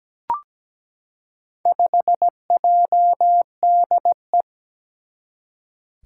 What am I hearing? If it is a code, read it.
Morse code, heard as a keyed tone, about 17 wpm, 700 Hz. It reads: 5JDE
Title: Four Challenging Letter-Number Combinations 17wpm